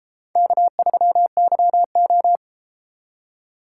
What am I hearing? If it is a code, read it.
Morse code, heard as a keyed tone, about 32 wpm, 700 Hz. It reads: K3YO